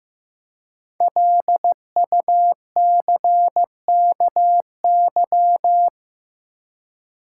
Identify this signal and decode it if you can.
Morse code, heard as a keyed tone, about 15 wpm, 700 Hz. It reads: LUCKY
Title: Five Letter Words - 15wpm